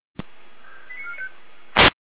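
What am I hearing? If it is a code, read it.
Five-tone selective call (ZVEI), heard as quick runs of five tones, tones of about 80 ms. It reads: 89495
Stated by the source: ZVEI five-tone selective call received on an SDR.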